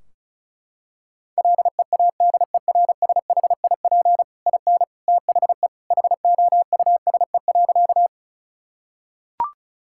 Morse code, heard as a keyed tone, about 36 wpm, 700 Hz. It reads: LEADERSHIP IN THE HOUSE.
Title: Sentences from Top 1600 Words 35wpm